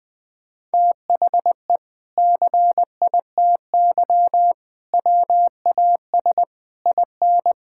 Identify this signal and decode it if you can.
Morse code, heard as a keyed tone, about 20 wpm, 700 Hz. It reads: THE CITY WAS IN
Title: Sentences from Top 1600 Words 20wpm